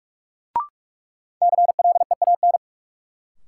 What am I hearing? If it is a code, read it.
Morse code, heard as a keyed tone, about 46 wpm, 700 Hz. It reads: CLEAN
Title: Sentences from Top 1100 Words - New Word Drill - 45wpm